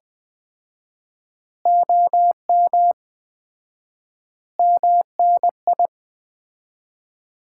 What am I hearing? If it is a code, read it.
Morse code, heard as a keyed tone, about 20 wpm, 700 Hz. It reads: OM MNI
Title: Common QSO Phrases - 4x Word-Spacing - 20wpm